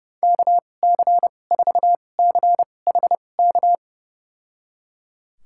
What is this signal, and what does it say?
Morse code, heard as a keyed tone, about 29 wpm, spaced as for 22 wpm, 700 Hz. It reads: KC4CHK
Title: USA Call Signs - 6 Characters - 2x Character Spacing - 30wpm